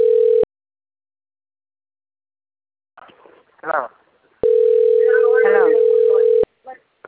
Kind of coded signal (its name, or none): telephone call progress tone